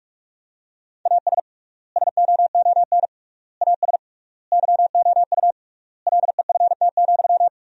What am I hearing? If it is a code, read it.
Morse code, heard as a keyed tone, about 46 wpm, 700 Hz. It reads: AS SOON AS YOU LEFT,
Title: 3x Word-Spacing - Sentences from Top 400 Words - 45wpm